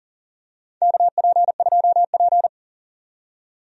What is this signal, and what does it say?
Morse code, heard as a keyed tone, about 39 wpm, 700 Hz. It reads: KP2P
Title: DXCC-100 Call Signs - 4 Characters - 40wpm